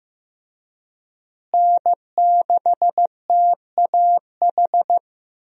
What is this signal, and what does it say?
Morse code, heard as a keyed tone, about 16 wpm, 700 Hz. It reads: N6TAH
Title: USA Call Signs - 5 Characters - Repeated 2x - 15wpm